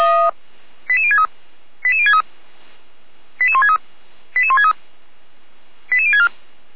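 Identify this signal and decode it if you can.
Five-tone selective call (ZVEI), heard as quick runs of five tones, tones of about 70 ms. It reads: 88063 88063 88163 88163 88064
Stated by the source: ZVEI five-tone selective call received on SDR.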